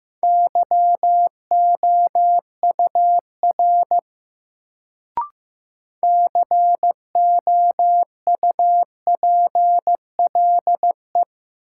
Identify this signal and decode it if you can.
Morse code, heard as a keyed tone, about 15 wpm, 700 Hz. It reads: YOUR COUPLE
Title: Sets of 2 Words - New Word Drill - 15wpm